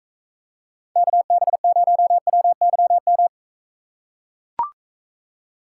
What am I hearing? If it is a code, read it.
Morse code, heard as a keyed tone, about 42 wpm, 700 Hz. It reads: KB0WYM